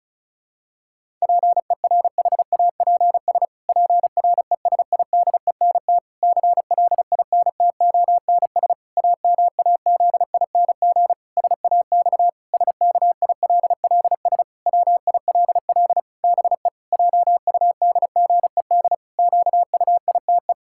Morse code, heard as a keyed tone, about 36 wpm, 700 Hz. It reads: PERHAPS PRESIDENT CLINTONS AMAZING SAX SKILLS WILL BE JUDGED QUITE
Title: Pangram Challenge v2 - 35wpm